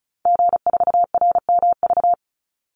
Morse code, heard as a keyed tone, about 34 wpm, 700 Hz. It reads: G4RMV
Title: Call Signs Repeated 3x - 35wpm